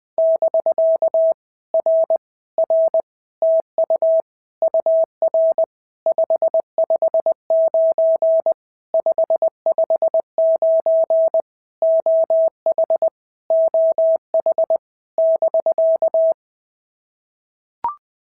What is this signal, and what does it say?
Morse code, heard as a keyed tone, about 20 wpm, 650 Hz. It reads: <BK> R R TU UR 559 559 OH OH <BK>
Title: POTA QSO Practice - 20wpm